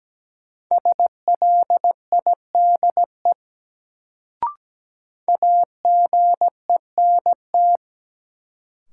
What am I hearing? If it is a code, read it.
Morse code, heard as a keyed tone, about 17 wpm, 700 Hz. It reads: SLIDE AGENT
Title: Sentences from Top 1600 Words - New Word Drill - 17wpm